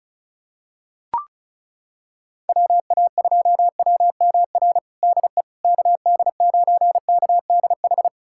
Morse code, heard as a keyed tone, about 36 wpm, 690 Hz. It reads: WA2WMR DE KD9KDH
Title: USA Callsign DE Callsign - 6 Characters - 35wpm